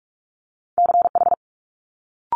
Morse code, heard as a keyed tone, about 46 wpm, 700 Hz. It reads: CH
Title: Multi-letter Phonemes Quick Fire 45wpm